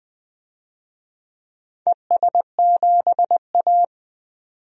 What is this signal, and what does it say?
Morse code, heard as a keyed tone, about 20 wpm, 700 Hz. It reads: ES7A
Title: DXCC-100 Call Signs - 4 Characters - Repeated 3x - 20wpm